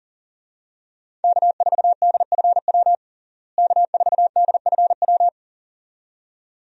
Morse code, heard as a keyed tone, about 40 wpm, 700 Hz. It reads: K4DFW K4DFW
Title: USA Call Signs - 5 Characters - Repeated 2x - 40wpm